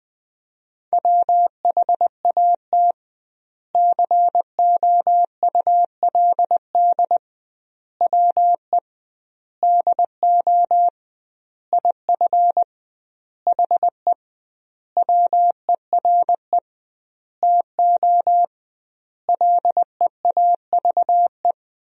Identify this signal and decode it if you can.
Morse code, heard as a keyed tone, about 20 wpm, 700 Hz. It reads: WHAT COULD WE DO IF HE WERE TO LEAVE
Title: 2x Word-Spacing - Sentences from Top 400 Words - 20wpm